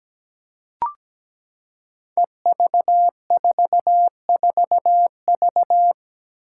Morse code, heard as a keyed tone, about 17 wpm, 700 Hz. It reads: EV44V
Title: Five Challenging Letter-Number Combinations 17wpm